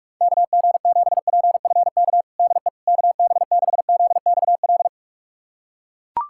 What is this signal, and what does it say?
Morse code, heard as a keyed tone, about 41 wpm, 690 Hz. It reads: KG7PFK DE KB6ZXL
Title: USA Callsign DE Callsign - 6 Characters - 45wpm